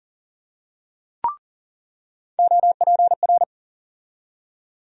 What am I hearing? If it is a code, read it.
Morse code, heard as a keyed tone, about 40 wpm, 700 Hz. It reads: OPR
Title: USA Call Sign Postfixes - 3 Letters 40wpm